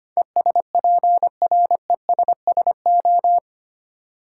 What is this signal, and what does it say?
Morse code, heard as a keyed tone, about 25 wpm, 700 Hz. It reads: ESPRESSO
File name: practice / 25wpm / Difficult Words with Hints - 25wpm